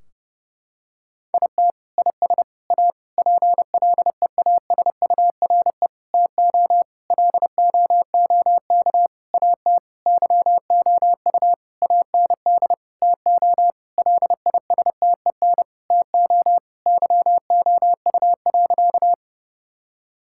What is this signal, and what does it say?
Morse code, heard as a keyed tone, about 30 wpm, 700 Hz. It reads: IT IS A PLEASURE TO LOOK AT YOU AND TO LISTEN TO YOU.
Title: Sentences from Top 1600 Words 30wpm